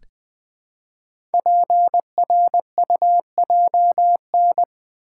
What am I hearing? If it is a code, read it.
Morse code, heard as a keyed tone, about 20 wpm, 700 Hz. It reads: PRUJN
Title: Code Groups - 5 Letters - 20wpm